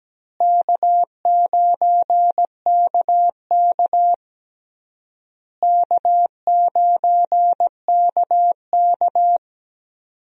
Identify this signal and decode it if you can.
Morse code, heard as a keyed tone, about 17 wpm, 700 Hz. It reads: K9KK K9KK
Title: USA Call Signs - 4 Characters - Repeated 3x - 17wpm